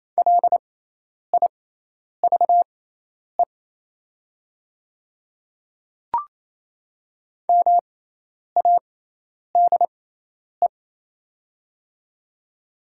Morse code, heard as a keyed tone, about 28 wpm, spaced as for 10 wpm, 700 Hz. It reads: LIVE MADE
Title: 6x Character Spacing - Top 300 Words - 28wpm